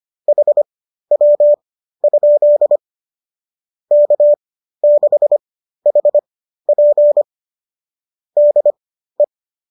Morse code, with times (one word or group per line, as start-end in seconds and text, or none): 0.3-2.8 HW?
3.9-7.2 K6HP
8.4-9.2 DE